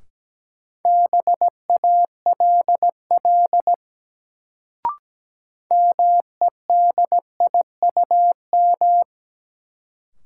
Morse code, heard as a keyed tone, about 17 wpm, 700 Hz. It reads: BALL MEDIUM
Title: Sentences from Top 900 Words - New Word Drill - 17wpm